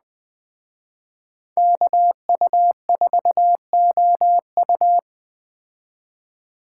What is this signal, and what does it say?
Morse code, heard as a keyed tone, about 20 wpm, 700 Hz. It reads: KU4OU
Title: Speed Racing - USA 2x2 Call Signs 17wpm